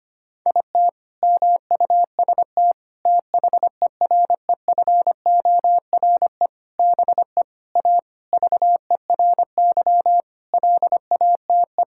Morse code, heard as a keyed tone, about 25 wpm, 700 Hz. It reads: IT MUST THEREFORE BE A VERY LATE
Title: Sentences from Top 1000 Words - Review Part 6 - 25wpm